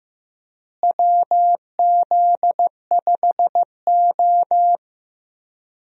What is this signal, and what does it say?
Morse code, heard as a keyed tone, about 15 wpm, 700 Hz. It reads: WZ5O
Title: USA Call Signs - 4 Characters - Repeated 3x - 15wpm